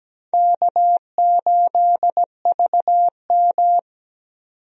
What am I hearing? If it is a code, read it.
Morse code, heard as a keyed tone, about 17 wpm, 700 Hz. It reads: K8VM
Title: USA Call Signs - 4 Characters - 17wpm